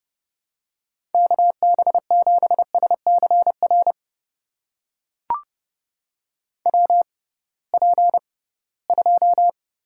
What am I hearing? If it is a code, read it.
Morse code, heard as a keyed tone, about 30 wpm, spaced as for 19 wpm, 700 Hz. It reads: KB7SCR WP2
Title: USA Call Signs - 6 Characters - 6x Character Spacing - 30wpm